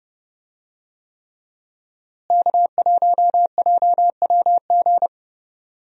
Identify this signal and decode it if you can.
Morse code, heard as a keyed tone, about 30 wpm, 700 Hz. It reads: K1JWG
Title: USA Call Signs - 5 Characters - Repeated 3x - 30wpm